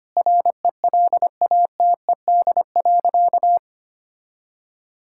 Morse code, heard as a keyed tone, about 25 wpm, 700 Hz. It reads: RELATED.
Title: Sentences from Top 1000 Words - Review Part 5 - 25wpm